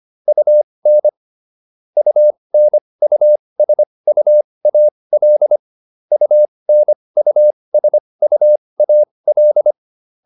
Morse code, heard as a keyed tone, about 25 wpm, spaced as for 20 wpm, 600 Hz. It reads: UN UNUSUAL UNUSUAL